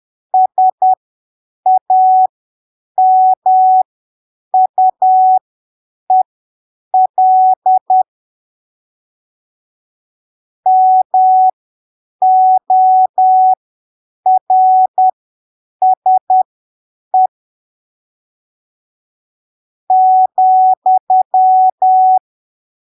Morse code, heard as a keyed tone, about 10 wpm, spaced as for 7 wpm, 750 Hz. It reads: SAMUEL MORSE ,